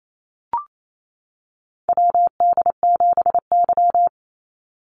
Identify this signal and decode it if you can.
Morse code, heard as a keyed tone, about 28 wpm, 700 Hz. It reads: WD7Y
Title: NA SOTA Chaser Callsigns 28wpm